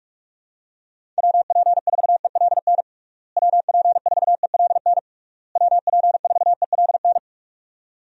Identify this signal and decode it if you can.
Morse code, heard as a keyed tone, about 46 wpm, 700 Hz. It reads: WP4ELN WP4ELN WP4ELN
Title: USA Call Signs - 6 Characters - Repeated 3x - 45wpm